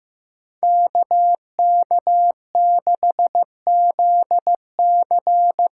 Morse code, heard as a keyed tone, about 15 wpm, 700 Hz. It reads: KK6ZC
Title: USA Call Signs - 8x Character Spacing - 15wpm